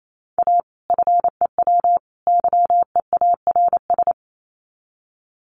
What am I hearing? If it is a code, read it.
Morse code, heard as a keyed tone, about 28 wpm, 700 Hz. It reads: A FEW YEARS
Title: Sets of 3 Words 28wpm